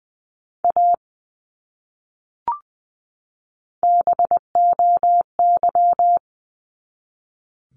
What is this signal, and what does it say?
Morse code, heard as a keyed tone, about 20 wpm, 700 Hz. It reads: A BOY
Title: Top 200 Words 20wpm